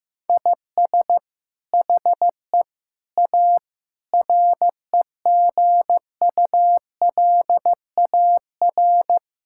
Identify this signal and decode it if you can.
Morse code, heard as a keyed tone, about 15 wpm, 700 Hz. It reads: IS HE A REGULAR
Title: Sentences from Top 1000 Words - Review Part 6 - 15wpm